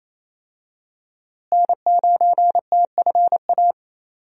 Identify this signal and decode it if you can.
Morse code, heard as a keyed tone, about 27 wpm, 700 Hz. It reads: N9TFA
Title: USA Call Signs - 5 Characters - Repeated 3x - 28wpm